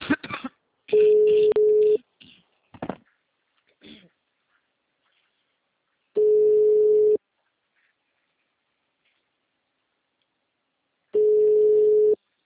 Ringback tone (Europe, 425 Hz).